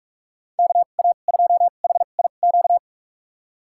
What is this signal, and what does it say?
Morse code, heard as a keyed tone, about 46 wpm, spaced as for 32 wpm, 700 Hz. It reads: KA2HIQ
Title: USA Call Signs - 2x Character Spacing - 45wpm